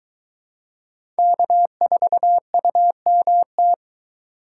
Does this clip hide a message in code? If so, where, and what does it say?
Yes: Morse code from 1.2 to 3.7 s, heard as a keyed tone, about 21 wpm, 700 Hz. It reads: K4UMT